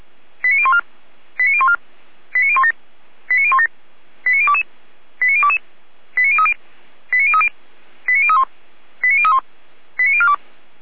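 Five-tone selective call (ZVEI), heard as quick runs of five tones, tones of about 70 ms. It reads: 89014 89014 89017 89017 89022 89022 89030 89030 89031 89031 89042
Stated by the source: Five-tone ZVEI selective call received on SDR.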